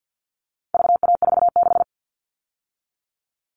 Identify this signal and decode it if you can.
Morse code, heard as a keyed tone, about 50 wpm, 700 Hz. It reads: VA46
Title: All Cut Numbers - 4-Digit 50wpm